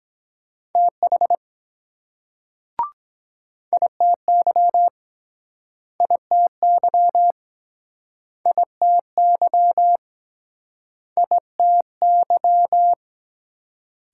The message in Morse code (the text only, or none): TH ITY ITY ITY ITY